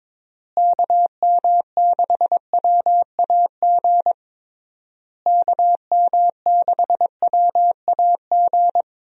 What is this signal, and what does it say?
Morse code, heard as a keyed tone, about 21 wpm, 700 Hz. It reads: KM6WAG KM6WAG
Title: USA Call Signs - 6 Characters - Repeated 3x - 22wpm